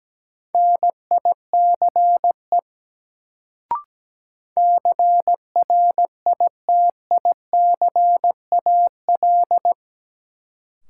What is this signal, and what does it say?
Morse code, heard as a keyed tone, about 17 wpm, 700 Hz. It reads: NICE CRITICAL
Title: Sentences from Top 1100 Words - New Word Drill - 17wpm